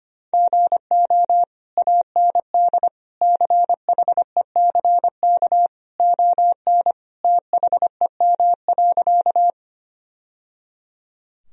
Morse code, heard as a keyed tone, about 25 wpm, 700 Hz. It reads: GO AND CHECK ON THEM.
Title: Sentences from Top 1000 Words - Review Part 3 - 25wpm